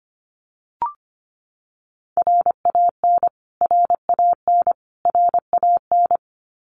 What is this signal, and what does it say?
Morse code, heard as a keyed tone, about 25 wpm, 700 Hz. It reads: RAN RAN RAN
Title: Top 500 Words Repeated 25wpm